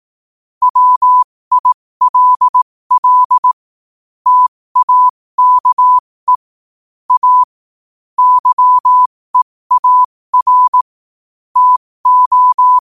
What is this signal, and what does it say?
Morse code, heard as a keyed tone, about 18 wpm, spaced as for 15 wpm, 1000 Hz. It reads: WILL TAKE A YEAR TO
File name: qotdfiles / 15wpm